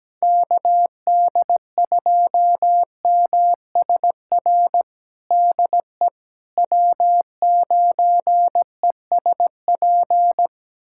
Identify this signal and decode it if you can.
Morse code, heard as a keyed tone, about 17 wpm, 700 Hz. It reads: KD2MSR DE W9ESP
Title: USA Callsign DE Callsign 17wpm